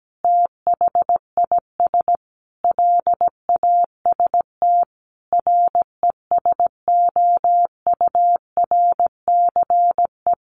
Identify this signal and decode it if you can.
Morse code, heard as a keyed tone, about 17 wpm, 700 Hz. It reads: THIS LAST RESOURCE
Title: Sentences from Top 1300 Words 17wpm